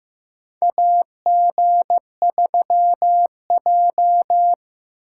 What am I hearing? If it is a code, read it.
Morse code, heard as a keyed tone, about 15 wpm, 700 Hz. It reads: AG3J
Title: USA Call Signs - 4 Characters - 2x Character Spacing - 15wpm